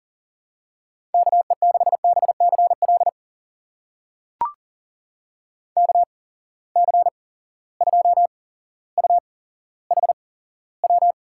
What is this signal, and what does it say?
Morse code, heard as a keyed tone, about 42 wpm, spaced as for 17 wpm, 700 Hz. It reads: KE6BCL KC2UHW
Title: USA Call Signs - 6 Characters - 8x Character Spacing - 40wpm